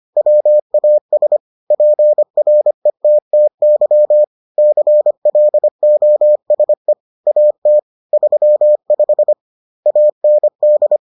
Morse code, heard as a keyed tone, about 25 wpm, 600 Hz. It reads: WAS PRETTY CLOSE AT 35 AND